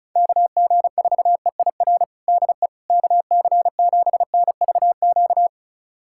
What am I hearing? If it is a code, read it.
Morse code, heard as a keyed tone, about 36 wpm, 700 Hz. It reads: KG4EIR DE KC7NVQ
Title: USA Callsign DE Callsign 35wpm